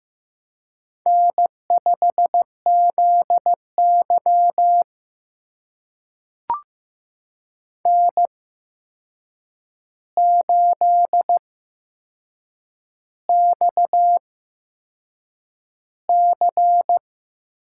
Morse code, heard as a keyed tone, about 15 wpm, spaced as for 6 wpm, 700 Hz. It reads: N5ZY N8XC
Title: USA Call Signs - 4 Characters - 8x Character Spacing - 15wpm